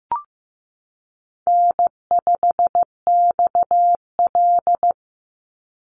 Morse code, heard as a keyed tone, about 15 wpm, 700 Hz. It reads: N5XL
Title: NA SOTA Chaser Callsigns 15wpm